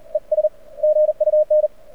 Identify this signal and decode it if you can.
Morse code, heard as a keyed tone, about 42 wpm, 590 Hz. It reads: ES GUN
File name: archived / CRSnet